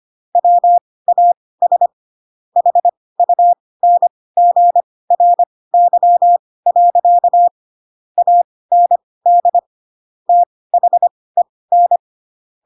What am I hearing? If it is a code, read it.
Morse code, heard as a keyed tone, about 25 wpm, spaced as for 18 wpm, 700 Hz. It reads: WAS HUNGRY. AND THEN
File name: Pinocchio / Pinocchio ch7 12 25 ik1hge